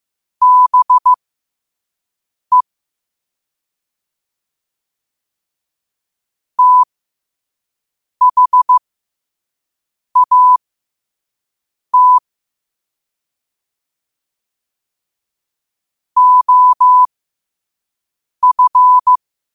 Morse code, heard as a keyed tone, about 15 wpm, spaced as for 5 wpm, 1000 Hz. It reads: BE THAT OF
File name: qotdfiles / 5wpm